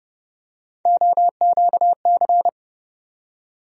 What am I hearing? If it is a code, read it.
Morse code, heard as a keyed tone, about 32 wpm, 700 Hz. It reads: OQC
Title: USA Call Sign Postfixes - 3 Letters 30wpm